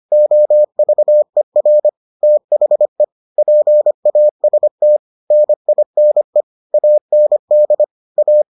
Morse code, heard as a keyed tone, about 25 wpm, 600 Hz. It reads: OVER THE PAST NINE AND A